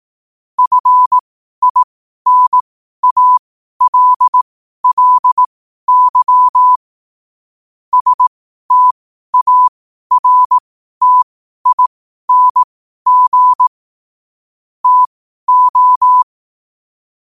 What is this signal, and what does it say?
Morse code, heard as a keyed tone, about 18 wpm, spaced as for 12 wpm, 1000 Hz. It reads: FINALLY STARTING TO
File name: qotdfiles / 12wpm